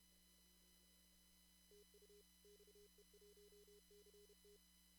Morse code, heard as a keyed tone, about 32 wpm, 410 Hz. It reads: TU XE1CT
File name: CW / kt5j wpx cw-op-k5pi